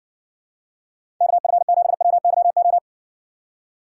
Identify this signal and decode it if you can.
Morse code, heard as a keyed tone, about 61 wpm, 700 Hz. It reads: KF7WYO